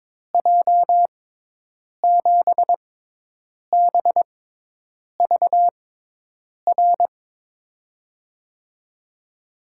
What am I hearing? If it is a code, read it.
Morse code, heard as a keyed tone, about 22 wpm, spaced as for 8 wpm, 700 Hz. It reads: J7BVR
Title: USA Call Signs - 6 Characters - 6x Character Spacing - 22wpm